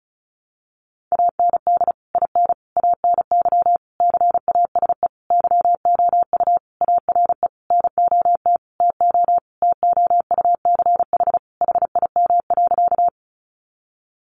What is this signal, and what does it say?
Morse code, heard as a keyed tone, about 34 wpm, 700 Hz. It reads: AND IN ANY CASE YOU ARE NOT TO TOUCH HIM.
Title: Sentences from Top 1200 Words 35wpm